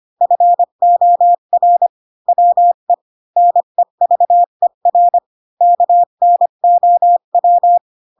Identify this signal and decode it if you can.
Morse code, heard as a keyed tone, about 25 wpm, 700 Hz. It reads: FOR WE NEVER KNOW
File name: Pinocchio / Pinocchio ch7 20 25 ik1hge